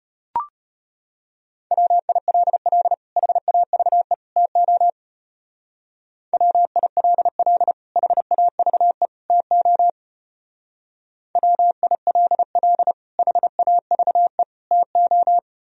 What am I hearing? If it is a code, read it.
Morse code, heard as a keyed tone, about 32 wpm, 700 Hz. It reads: WILL HAVE TO WILL HAVE TO WILL HAVE TO